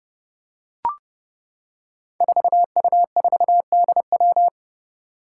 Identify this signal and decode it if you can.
Morse code, heard as a keyed tone, about 30 wpm, 700 Hz. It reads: 4U4DW
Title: Five Challenging Letter-Number Combinations 30wpm